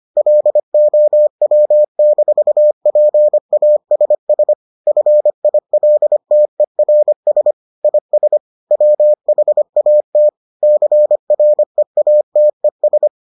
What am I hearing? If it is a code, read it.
Morse code, heard as a keyed tone, about 25 wpm, 600 Hz. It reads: LOW-PASS FILTERS IS WHAT CREATES